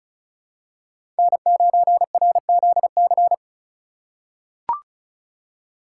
Morse code, heard as a keyed tone, about 36 wpm, 700 Hz. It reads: N9RZC
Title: USA Call Signs - 5 Characters - 3x Character Spacing - 35wpm